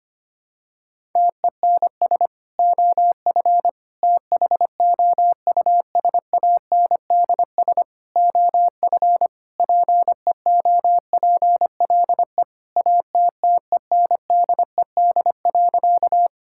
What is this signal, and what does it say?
Morse code, heard as a keyed tone, about 25 wpm, 700 Hz. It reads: TENS OF THOUSANDS OF PEOPLE ATTENDED.
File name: practice / 25wpm / Sentences with 4-Word Phrases - 25wpm